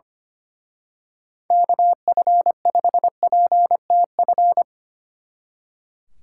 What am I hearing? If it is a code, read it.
Morse code, heard as a keyed tone, about 26 wpm, 700 Hz. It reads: KF5PTF
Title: Speed Racing - USA 2x3 Call Signs 25wpm